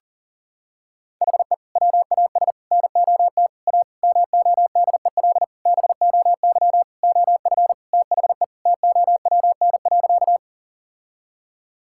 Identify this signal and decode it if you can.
Morse code, heard as a keyed tone, about 40 wpm, 700 Hz. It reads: HE WAS NOT A MODEL BOY OF THE TOWN.
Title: Sentences from Top 1000 Words - Review Part 5 - 40wpm